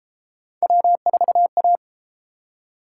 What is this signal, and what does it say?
Morse code, heard as a keyed tone, about 29 wpm, 700 Hz. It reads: W4A